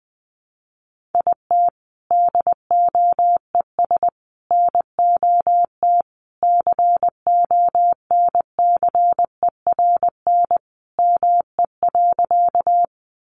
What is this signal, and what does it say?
Morse code, heard as a keyed tone, about 20 wpm, 700 Hz. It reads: IT DOES NOT CONCERN ME.
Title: Sentences from Top 1000 Words 20wpm